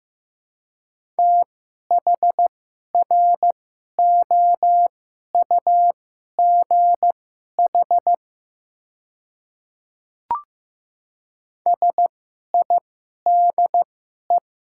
Morse code, heard as a keyed tone, about 15 wpm, spaced as for 11 wpm, 690 Hz. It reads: THROUGH SIDE
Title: 2x Character Spacing - Top 300 Words - 15wpm